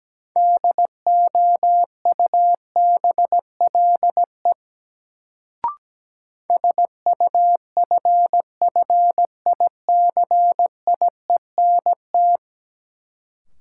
Morse code, heard as a keyed tone, about 17 wpm, 700 Hz. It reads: DOUBLE SUFFICIENT
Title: Sentences from Top 1400 Words - New Word Drill - 17wpm